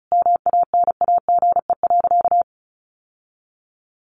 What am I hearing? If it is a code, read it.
Morse code, heard as a keyed tone, about 34 wpm, 700 Hz. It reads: MANAGE.
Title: Sentences from Top 1400 Words 35wpm